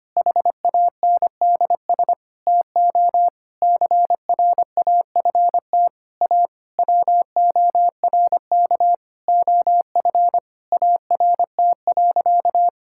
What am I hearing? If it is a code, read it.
Morse code, heard as a keyed tone, about 25 wpm, 700 Hz. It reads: HANDS TO CRAFT A WORK OF ART.
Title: Sentences from Top 1000 Words - Review Part 6 - 25wpm